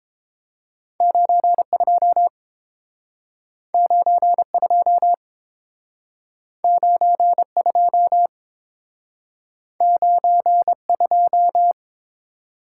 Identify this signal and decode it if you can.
Morse code, heard as a keyed tone, about 29 wpm, 700 Hz. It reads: 92 92 92 92